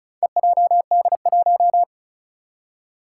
Morse code, heard as a keyed tone, about 35 wpm, 700 Hz. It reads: EJD1
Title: Four Challenging Letter-Number Combinations 35wpm